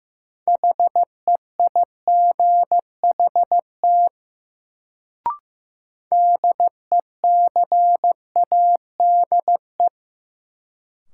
Morse code, heard as a keyed tone, about 15 wpm, 700 Hz. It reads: HEIGHT DECADE
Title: Sets of 5 Words - New Word Drill - 15wpm